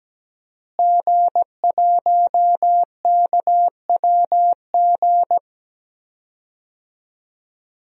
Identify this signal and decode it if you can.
Morse code, heard as a keyed tone, about 17 wpm, 700 Hz. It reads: G1KWG
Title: DXCC-100 Call Signs - 5 Characters - Repeated 2x - 17wpm